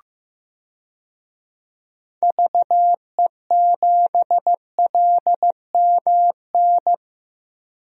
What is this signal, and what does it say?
Morse code, heard as a keyed tone, about 15 wpm, 700 Hz. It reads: VE7LMN
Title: DXCC-100 Call Signs - 6 Characters - 15wpm